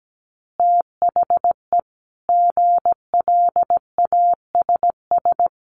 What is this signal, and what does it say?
Morse code, heard as a keyed tone, about 17 wpm, 700 Hz. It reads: THE GLASS
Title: Additional Easy Phrases 17wpm